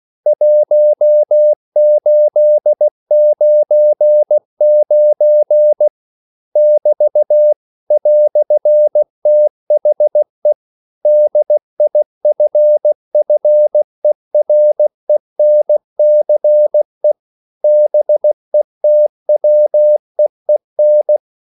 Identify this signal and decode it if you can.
Morse code, heard as a keyed tone, about 16 wpm, 600 Hz. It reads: 1899 <BT> "THE DIFFERENCE BETWEEN
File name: CW / Practice / EnglishQuotations0416